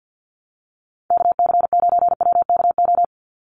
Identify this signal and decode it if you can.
Morse code, heard as a keyed tone, about 50 wpm, 700 Hz. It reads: KC9WKO
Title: USA Call Signs 50wpm